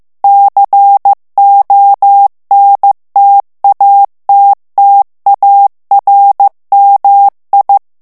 Morse code, heard as a keyed tone, about 15 wpm, 800 Hz. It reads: CONTATTARMI
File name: ITALIEN texte clair 1 vitesse 14 mots minute